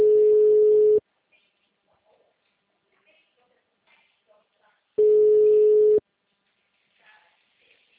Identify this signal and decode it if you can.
Ringback tone (Europe, 425 Hz), with speech.